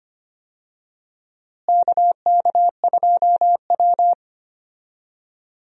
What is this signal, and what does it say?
Morse code, heard as a keyed tone, about 25 wpm, 700 Hz. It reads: KK2W
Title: USA Call Signs - 4 Characters - Repeated 2x - 25wpm